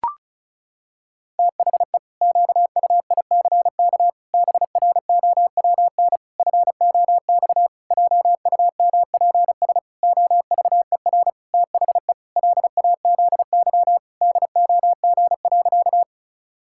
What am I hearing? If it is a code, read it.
Morse code, heard as a keyed tone, about 36 wpm, 700 Hz. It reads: THE QUICK BROWN FOX JUMPS OVER THE LAZY DOG.
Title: Pangram Challenge v2 - 35wpm